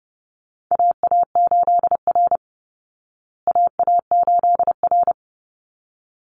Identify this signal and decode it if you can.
Morse code, heard as a keyed tone, about 30 wpm, 700 Hz. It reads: AA8R AA8R
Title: Call Signs Repeated 3x - 30wpm